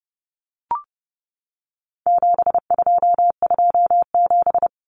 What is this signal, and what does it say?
Morse code, heard as a keyed tone, about 30 wpm, 700 Hz. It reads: 7227
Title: Common Cut Numbers - 4-Digit 30wpm